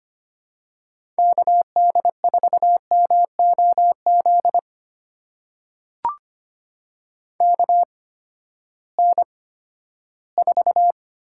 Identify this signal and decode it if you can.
Morse code, heard as a keyed tone, about 25 wpm, spaced as for 14 wpm, 700 Hz. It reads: KD4MOZ KN4
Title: USA Call Signs - 6 Characters - 8x Character Spacing - 25wpm